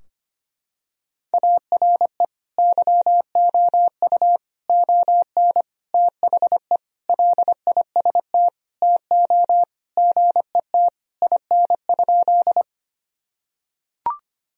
Morse code, heard as a keyed tone, about 25 wpm, 700 Hz. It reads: ARE YOU ON THE LIST TO GET IN?
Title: Sentences from Top 1000 Words - Review Part 2 - 25wpm